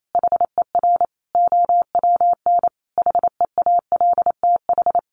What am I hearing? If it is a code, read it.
Morse code, heard as a keyed tone, about 28 wpm, 700 Hz. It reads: HER OWN HEALTH
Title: Sentences from Top 1100 Words 28wpm